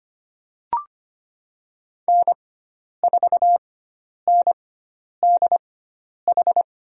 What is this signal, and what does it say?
Morse code, heard as a keyed tone, about 25 wpm, spaced as for 10 wpm, 700 Hz. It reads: N4NDH
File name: Lesson 16g - Letter c - Call Signs 25wpm